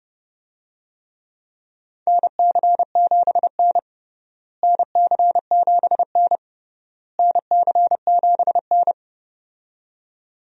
Morse code, heard as a keyed tone, about 32 wpm, 700 Hz. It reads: NC7N NC7N NC7N
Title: USA Call Signs - 4 Characters - Repeated 3x - 30wpm